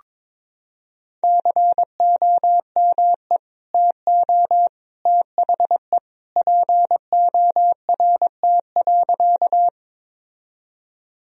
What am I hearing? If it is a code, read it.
Morse code, heard as a keyed tone, about 22 wpm, 700 Hz. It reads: COME TO THE PORT.
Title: Sentences from Top 1000 Words - Review Part 1 - 22wpm